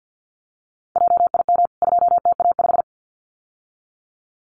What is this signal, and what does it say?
Morse code, heard as a keyed tone, about 50 wpm, 700 Hz. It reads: JIM 2TA5
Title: CWT Exchange 50wpm